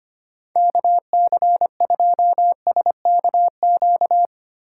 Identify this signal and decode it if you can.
Morse code, heard as a keyed tone, about 25 wpm, 700 Hz. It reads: KC2SKQ
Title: USA Call Signs - 6 Characters - 4x Character Spacing - 25wpm